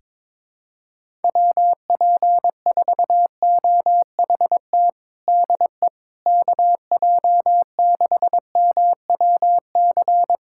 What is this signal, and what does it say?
Morse code, heard as a keyed tone, about 22 wpm, 700 Hz. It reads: WP4OHT DE KJ6MWC
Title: USA Callsign DE Callsign 22wpm